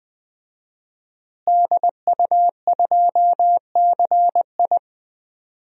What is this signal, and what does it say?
Morse code, heard as a keyed tone, about 21 wpm, 700 Hz. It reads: DU2CI
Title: DXCC-100 Call Signs - 5 Characters - Repeated 3x - 20wpm